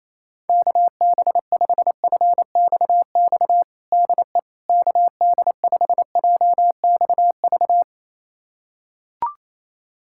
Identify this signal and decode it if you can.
Morse code, heard as a keyed tone, about 28 wpm, 700 Hz. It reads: KB5FXX DE KD5JXV
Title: USA Callsign DE Callsign - 6 Characters - 28wpm